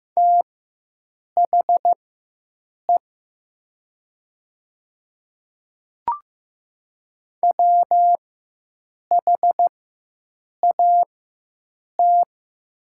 Morse code, heard as a keyed tone, about 15 wpm, spaced as for 7 wpm, 700 Hz. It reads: THE WHAT
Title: 4x Character Spacing - Top 200 Words - 15wpm